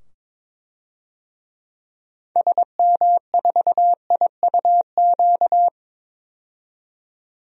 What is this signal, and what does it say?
Morse code, heard as a keyed tone, about 22 wpm, 700 Hz. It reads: SM4IUQ
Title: DXCC-100 Call Signs - 6 Characters - 22wpm